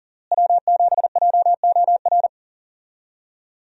Morse code, heard as a keyed tone, about 40 wpm, 700 Hz. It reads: W7JOR
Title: USA Call Signs - 5 Characters - Repeated 2x - 40wpm